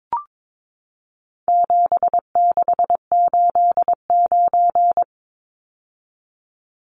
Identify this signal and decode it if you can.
Morse code, heard as a keyed tone, about 22 wpm, 700 Hz. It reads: 7689
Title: Numbers - 4-Digit 22wpm